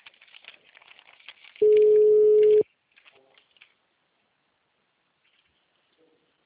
Ringback tone (Europe, 425 Hz).
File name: out- -2004- - - .74682